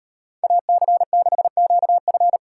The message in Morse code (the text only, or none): AC6QF